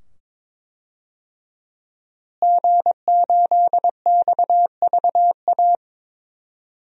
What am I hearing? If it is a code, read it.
Morse code, heard as a keyed tone, about 22 wpm, 700 Hz. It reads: G8XVA
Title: DXCC-100 Call Signs - 5 Characters - Repeated 2x - 22wpm